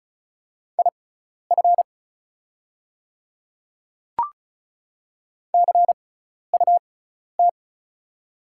Morse code, heard as a keyed tone, about 34 wpm, spaced as for 12 wpm, 700 Hz. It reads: IF CUT
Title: 6x Character Spacing - Top 300 Words - 35wpm